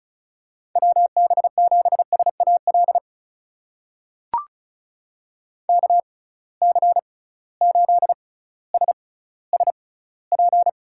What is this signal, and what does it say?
Morse code, heard as a keyed tone, about 36 wpm, spaced as for 18 wpm, 700 Hz. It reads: WB7SAL KC8SSP
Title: USA Call Signs - 6 Characters - 6x Character Spacing - 35wpm